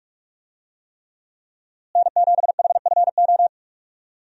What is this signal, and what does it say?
Morse code, heard as a keyed tone, about 41 wpm, 700 Hz. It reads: N7HFO
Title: USA Call Signs - 5 Characters - Repeated 3x - 45wpm